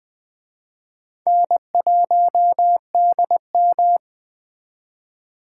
Morse code, heard as a keyed tone, about 21 wpm, 700 Hz. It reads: N1DM